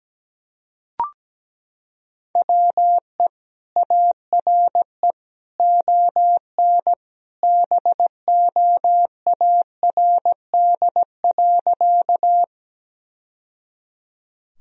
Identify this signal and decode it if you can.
Morse code, heard as a keyed tone, about 17 wpm, 700 Hz. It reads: WE ARE ON BOARD.
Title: Sentences from Top 1000 Words - Review Part 4 - 17wpm